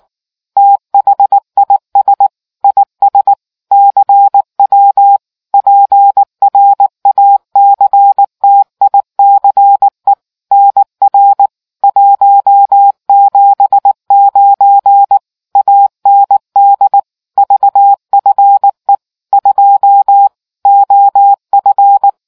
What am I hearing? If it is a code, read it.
Morse code, heard as a keyed tone, about 18 wpm, 800 Hz. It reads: THIS IS CW PRACTICE NR 179 AND VFE 2 OF